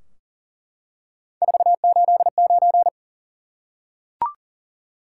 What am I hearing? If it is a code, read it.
Morse code, heard as a keyed tone, about 39 wpm, 700 Hz. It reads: 489
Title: Top 180 QSO Elements 40wpm